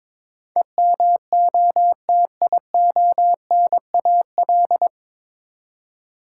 Morse code, heard as a keyed tone, about 22 wpm, 700 Hz. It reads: EMOTIONAL
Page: Morse Code Ninja practice 22wpm